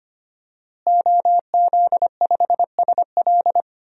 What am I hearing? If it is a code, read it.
Morse code, heard as a keyed tone, about 25 wpm, 700 Hz. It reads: OZ5SL
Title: DXCC-100 Call Signs - 5 Characters - Repeated 2x - 25wpm